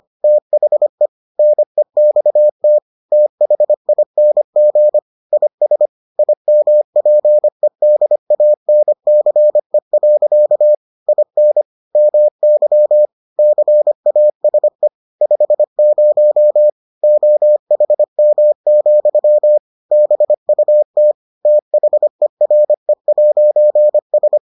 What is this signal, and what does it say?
Morse code, heard as a keyed tone, about 25 wpm, 600 Hz. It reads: THE NEXT THING IS IMPEDANCE. IN MY CASE 50 OHM, BUT THERE'S